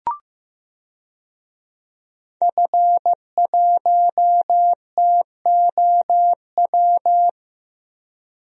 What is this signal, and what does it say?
Morse code, heard as a keyed tone, about 15 wpm, 700 Hz. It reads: F1TOW
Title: DXCC-100 Call Signs - 5 Characters - Repeated 2x - 15wpm